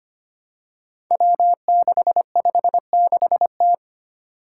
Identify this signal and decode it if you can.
Morse code, heard as a keyed tone, about 25 wpm, 700 Hz. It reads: W656T
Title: Five Challenging Letter-Number Combinations 25wpm